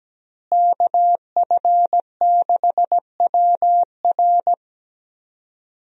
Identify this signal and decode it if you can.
Morse code, heard as a keyed tone, about 17 wpm, 700 Hz. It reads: KF6WR
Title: USA Call Signs - 5 Characters - Repeated 2x - 17wpm